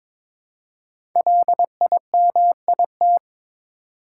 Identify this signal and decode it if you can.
Morse code, heard as a keyed tone, about 23 wpm, 700 Hz. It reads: LIMIT
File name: 5 Letter Words - First 3 Letters Given - 22wpm